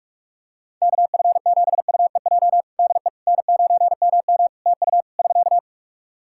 Morse code, heard as a keyed tone, about 48 wpm, 700 Hz. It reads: KF7UEJ DE N9MM TU <SK>
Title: Practice QSO - Exchange 4 - Ending - 45wpm